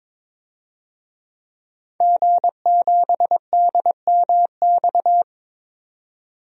Morse code, heard as a keyed tone, about 22 wpm, 700 Hz. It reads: G7DMX
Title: DXCC-100 Call Signs - 5 Characters - 22wpm